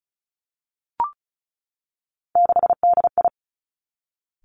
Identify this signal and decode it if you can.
Morse code, heard as a keyed tone, about 36 wpm, 700 Hz. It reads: 6DI
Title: Three Challenging Letter-Number Combinations 35wpm